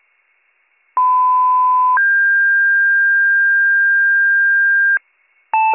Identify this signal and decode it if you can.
Two-tone page, heard as a one-second tone, then a three-second tone, about 1010 then 1600 Hz. It is heard from 1.0 to 5.0 s.